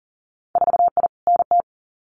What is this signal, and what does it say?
Morse code, heard as a keyed tone, about 40 wpm, 700 Hz. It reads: 4I NT
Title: Winter Field Day Exchanges Encore 40wpm